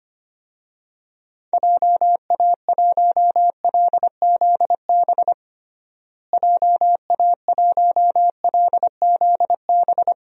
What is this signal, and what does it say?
Morse code, heard as a keyed tone, about 25 wpm, 700 Hz. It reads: JA1LZB JA1LZB
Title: DXCC-100 Call Signs - 6 Characters - Repeated 2x - 25wpm